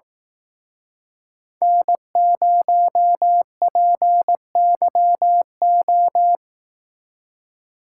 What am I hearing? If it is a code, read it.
Morse code, heard as a keyed tone, about 18 wpm, 700 Hz. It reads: N0PYO
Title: Speed Racing - USA 1x3 Call Signs 15wpm